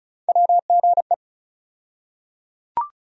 Morse code, heard as a keyed tone, about 34 wpm, 700 Hz. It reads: WGE
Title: USA Call Sign Postfixes - 3 Letters 35wpm